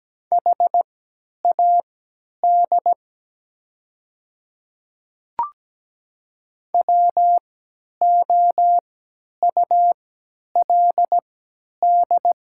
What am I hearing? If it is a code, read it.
Morse code, heard as a keyed tone, about 17 wpm, spaced as for 10 wpm, 700 Hz. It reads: HAD WOULD
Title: 3x Character Spacing - Top 300 Words - 17wpm